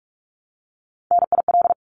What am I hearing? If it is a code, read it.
Morse code, heard as a keyed tone, about 44 wpm, 700 Hz. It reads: NIL
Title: Abbreviations Repeated 3x - 45wpm